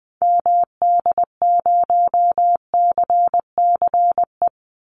Morse code, heard as a keyed tone, about 20 wpm, 700 Hz. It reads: MD0CCE
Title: Call Signs Repeated 2x - 20wpm